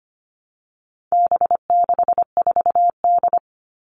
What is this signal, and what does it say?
Morse code, heard as a keyed tone, about 25 wpm, 700 Hz. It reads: B64D
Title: All Cut Numbers - 4-Digit 25wpm